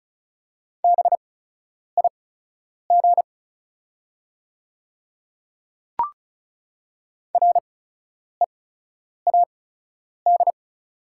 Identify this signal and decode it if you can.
Morse code, heard as a keyed tone, about 34 wpm, spaced as for 10 wpm, 690 Hz. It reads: BIG READ
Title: 8x Character Spacing - Top 200 Words - 35wpm